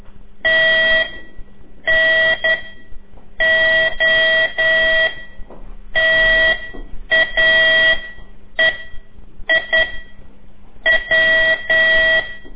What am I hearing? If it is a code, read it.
Morse code, heard as a keyed tone, about 10 wpm, spaced as for 8 wpm, 1950 Hz. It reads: T N O TAE I W
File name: CORSI CW